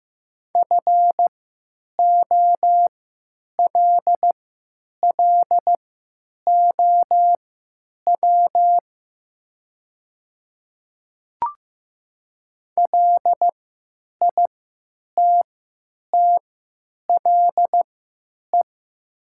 Morse code, heard as a keyed tone, about 15 wpm, spaced as for 8 wpm, 700 Hz. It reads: FOLLOW LITTLE
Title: 3x Character Spacing - Top 200 Words - 15wpm